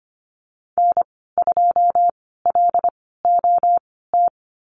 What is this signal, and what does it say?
Morse code, heard as a keyed tone, about 25 wpm, spaced as for 16 wpm, 700 Hz. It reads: N2LOT
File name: Lesson 15g - Number 5 - Call Signs 25wpm